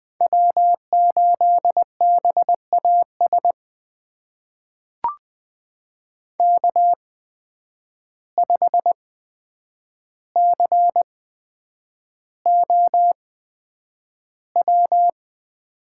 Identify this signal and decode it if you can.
Morse code, heard as a keyed tone, about 20 wpm, spaced as for 9 wpm, 700 Hz. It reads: W8BAS K5COW
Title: USA Call Signs - 5 Characters - 8x Character Spacing - 20wpm